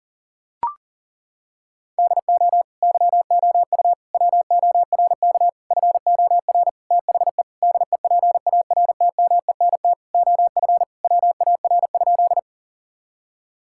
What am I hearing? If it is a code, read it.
Morse code, heard as a keyed tone, about 40 wpm, 700 Hz. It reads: DO YOU WORK FOR THE DEPARTMENT OF WAR?
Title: Sentences from Top 1000 Words - Review Part 5 - 40wpm